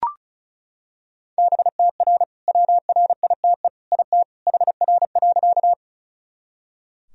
Morse code, heard as a keyed tone, about 34 wpm, 700 Hz. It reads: BTR WRITE IT HR.
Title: Sentences from Top 400 Words and Abbreviations 35wpm